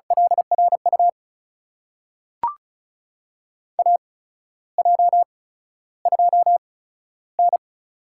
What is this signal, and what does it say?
Morse code, heard as a keyed tone, about 36 wpm, spaced as for 13 wpm, 700 Hz. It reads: LRU AJ2N
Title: USA Call Signs - 8x Character Spacing - 35wpm